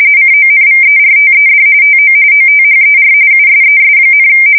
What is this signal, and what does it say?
RTTY, about 45.45 baud. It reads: CONVERTED TO OTHER USES,
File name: fsk170 / battleshipnj